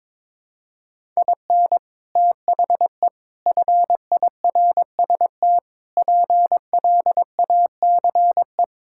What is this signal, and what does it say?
Morse code, heard as a keyed tone, about 22 wpm, 700 Hz. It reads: IN THE FIRST PLACE
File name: practice / 22wpm / Sentences with 4-Word Phrases - 22wpm